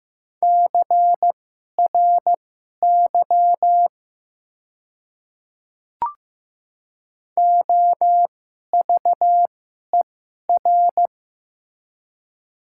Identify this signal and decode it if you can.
Morse code, heard as a keyed tone, about 15 wpm, spaced as for 11 wpm, 700 Hz. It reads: CRY OVER
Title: 2x Character Spacing - Top 500 Words - 15wpm